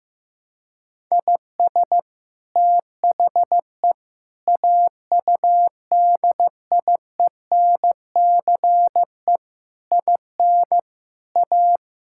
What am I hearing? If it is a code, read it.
Morse code, heard as a keyed tone, about 16 wpm, 700 Hz. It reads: IS THE AUDIENCE IN A
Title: Sentences from Top 1000 Words - Review Part 5 - 15wpm